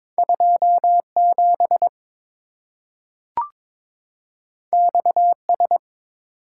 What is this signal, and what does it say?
Morse code, heard as a keyed tone, about 22 wpm, 690 Hz. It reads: 27 XS
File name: practice / 22wpm / Two Letter-Number Combinations - 22wpm